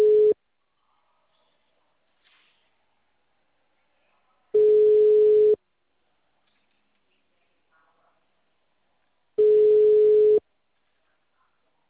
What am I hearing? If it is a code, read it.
Ringback tone (Europe, 425 Hz).